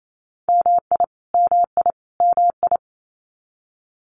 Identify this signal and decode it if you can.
Morse code, heard as a keyed tone, about 28 wpm, 700 Hz. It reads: MI MI MI
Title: US States Repeated 3x 28wpm